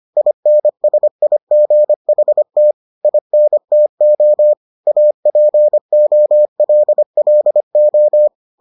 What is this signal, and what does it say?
Morse code, heard as a keyed tone, about 25 wpm, 600 Hz. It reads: INSIGHT INTO APOLLO